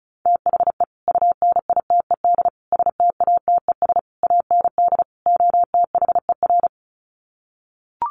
Morse code, heard as a keyed tone, about 34 wpm, 700 Hz. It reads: THE UNITED STATES AND OTHER
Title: Sets of 5 Words Encore 35wpm